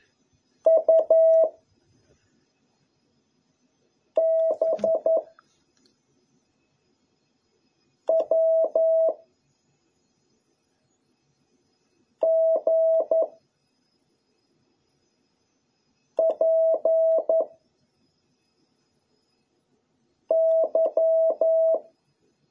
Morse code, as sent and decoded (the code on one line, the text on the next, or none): ..- / -... / .-- / --. / .--. / -.--
U B W G P Y